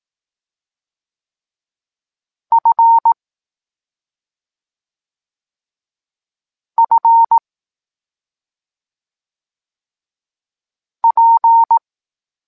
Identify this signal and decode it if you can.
Morse code, heard as a keyed tone, about 18 wpm, 920 Hz. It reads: F F P
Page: Tekst 25 – Nauka liter P F